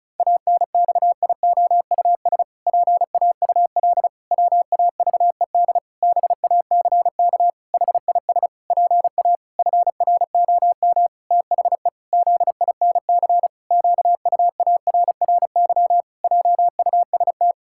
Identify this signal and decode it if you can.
Morse code, heard as a keyed tone, about 36 wpm, 700 Hz. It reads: ANXIOUS PAUL WAVED BACK HIS PA FROM THE ZINC QUARRY JUST
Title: Pangram Challenge v2 - 35wpm